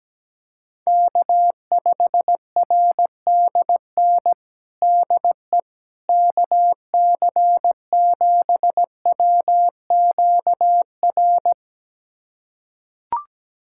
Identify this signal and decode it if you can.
Morse code, heard as a keyed tone, about 17 wpm, 700 Hz. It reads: K5RDN DE KC7WQR
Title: USA Callsign DE Callsign 17wpm